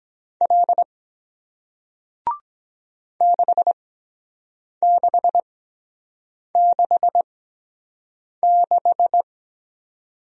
Morse code, heard as a keyed tone, about 23 wpm, spaced as for 7 wpm, 700 Hz. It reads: L 6666